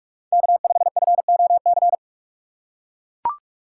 Morse code, heard as a keyed tone, about 41 wpm, 700 Hz. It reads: K5FOC
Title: USA Call Signs - 5 Characters - 3x Character Spacing - 45wpm